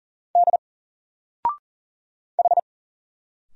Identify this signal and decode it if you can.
Morse code, heard as a keyed tone, about 39 wpm, 700 Hz. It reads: D H
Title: Single Letter-Number Alt 40wpm